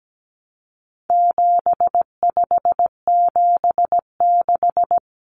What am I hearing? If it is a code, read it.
Morse code, heard as a keyed tone, about 17 wpm, 700 Hz. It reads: 7576
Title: Common Cut Numbers - 4-Digit 17wpm